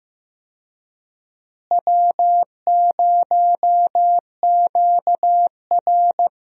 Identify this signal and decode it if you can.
Morse code, heard as a keyed tone, about 15 wpm, 700 Hz. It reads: W0QR
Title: USA Call Signs - 4 Characters - Repeated 2x - 15wpm